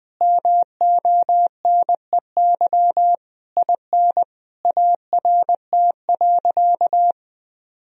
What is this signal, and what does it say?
Morse code, heard as a keyed tone, about 20 wpm, 700 Hz. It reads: MONEY IN ART.
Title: Sentences from Top 1000 Words - Review Part 3 - 20wpm